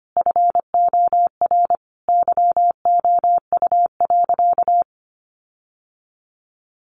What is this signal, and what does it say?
Morse code, heard as a keyed tone, about 25 wpm, 700 Hz. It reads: FOR YOU.
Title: Sentences from Top 1500 Words 25wpm